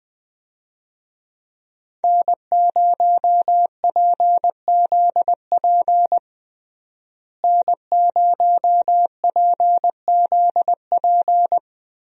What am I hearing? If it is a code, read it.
Morse code, heard as a keyed tone, about 21 wpm, 700 Hz. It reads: N0PZP N0PZP
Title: USA Call Signs - 5 Characters - Repeated 3x - 20wpm